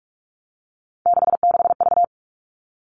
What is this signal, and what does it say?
Morse code, heard as a keyed tone, about 44 wpm, 700 Hz. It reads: 66V
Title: All Cut Numbers - 3-Digit 45wpm